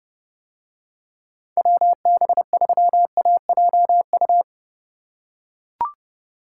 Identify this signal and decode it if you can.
Morse code, heard as a keyed tone, about 30 wpm, 700 Hz. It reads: WB3AJU